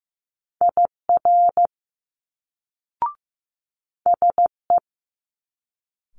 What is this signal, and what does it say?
Morse code, heard as a keyed tone, about 15 wpm, 700 Hz. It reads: IR SE
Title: Multi-letter Phonemes Quick Fire 15wpm